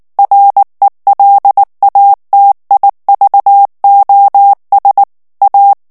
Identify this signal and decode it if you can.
Morse code, heard as a keyed tone, about 19 wpm, 800 Hz. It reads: RELATIVOS A
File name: morse mp3 / ESPAGNOL TEXTE EN CLAIR 2 VITESSE 18 MOTS MINUTES